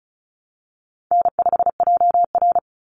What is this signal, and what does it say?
Morse code, heard as a keyed tone, about 34 wpm, 700 Hz. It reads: N5JR
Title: Easy Call Signs Repeated 3x - 35wpm